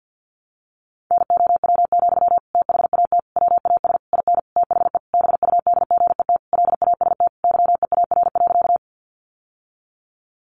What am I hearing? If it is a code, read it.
Morse code, heard as a keyed tone, about 50 wpm, 700 Hz. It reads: NOW, THAT WAS IN THE BUDGET LAST YEAR.
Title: Sentences from Top 1200 Words 50wpm